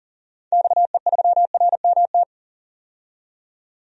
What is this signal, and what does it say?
Morse code, heard as a keyed tone, about 38 wpm, 700 Hz. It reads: XE3RMT
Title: DXCC-100 Call Signs - 6 Characters - 40wpm